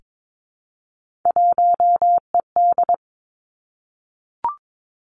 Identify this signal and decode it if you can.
Morse code, heard as a keyed tone, about 22 wpm, 700 Hz. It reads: 1ED
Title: Three Challenging Letter-Number Combinations 22wpm